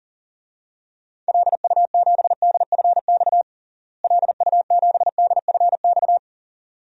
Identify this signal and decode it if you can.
Morse code, heard as a keyed tone, about 37 wpm, 700 Hz. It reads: LU7DFX LU7DFX
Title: DXCC-100 Call Signs - 6 Characters - Repeated 2x - 40wpm